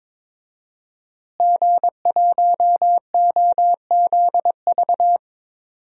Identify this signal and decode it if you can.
Morse code, heard as a keyed tone, about 21 wpm, 700 Hz. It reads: G1OZV
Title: DXCC-100 Call Signs - 5 Characters - Repeated 2x - 22wpm